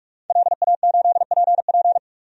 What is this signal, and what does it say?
Morse code, heard as a keyed tone, about 41 wpm, 700 Hz. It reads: LA8PP
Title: DXCC-100 Call Signs - 5 Characters - 45wpm